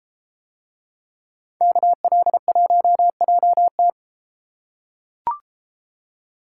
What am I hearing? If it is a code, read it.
Morse code, heard as a keyed tone, about 33 wpm, 700 Hz. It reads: KL1JT